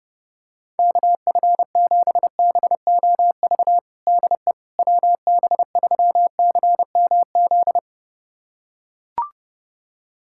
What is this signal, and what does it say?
Morse code, heard as a keyed tone, about 30 wpm, 700 Hz. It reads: KF7BOV DE WB3CMZ
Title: USA Callsign DE Callsign 30wpm